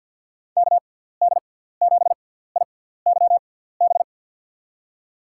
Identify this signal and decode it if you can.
Morse code, heard as a keyed tone, about 51 wpm, spaced as for 17 wpm, 700 Hz. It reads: KD7IYB
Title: USA Call Signs - 6 Characters - 6x Character Spacing - 50wpm